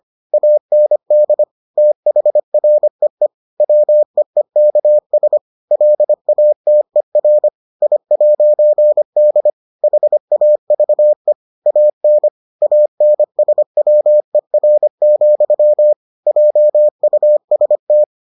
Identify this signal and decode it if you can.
Morse code, heard as a keyed tone, about 25 wpm, 600 Hz. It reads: AND THREE WEEKS LATER I'D HAVE AN ANSWER, JUST